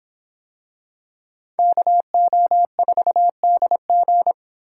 Morse code, heard as a keyed tone, about 24 wpm, 700 Hz. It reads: KO4DG